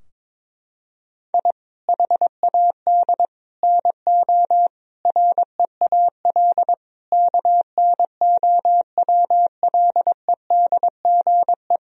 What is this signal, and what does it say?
Morse code, heard as a keyed tone, about 21 wpm, 700 Hz. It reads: I HAD NO REAL KNOWLEDGE
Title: Sentences from Top 1000 Words - Review Part 3 - 22wpm